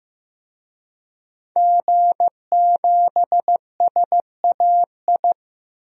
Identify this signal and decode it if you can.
Morse code, heard as a keyed tone, about 15 wpm, 700 Hz. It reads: G7SAI
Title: DXCC-100 Call Signs - 5 Characters - 15wpm